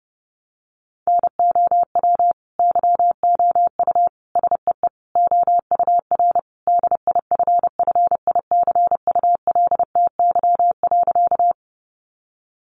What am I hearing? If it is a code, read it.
Morse code, heard as a keyed tone, about 30 wpm, 700 Hz. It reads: NOW YOU SEE OUR DIFFICULTY.
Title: Sentences from Top 1500 Words 30wpm